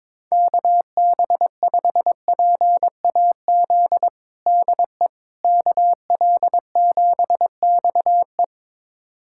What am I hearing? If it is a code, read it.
Morse code, heard as a keyed tone, about 22 wpm, 700 Hz. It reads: KB5PAZ DE KL7XE
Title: USA Callsign DE Callsign 22wpm